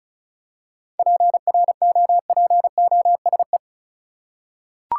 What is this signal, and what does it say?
Morse code, heard as a keyed tone, about 34 wpm, 700 Hz. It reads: PROPOSE
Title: Seven Letter Words - 35wpm